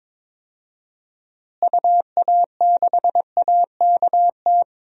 Morse code, heard as a keyed tone, about 22 wpm, 700 Hz. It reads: UA6AKT
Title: DXCC-100 Call Signs - 6 Characters - Repeated 3x - 22wpm